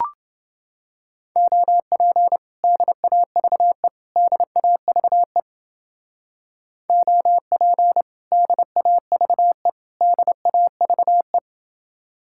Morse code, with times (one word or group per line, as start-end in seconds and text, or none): 1.4-2.4 OP
2.6-3.9 DAVE
4.2-5.4 DAVE
6.9-8.0 OP
8.3-9.7 DAVE
10.0-11.4 DAVE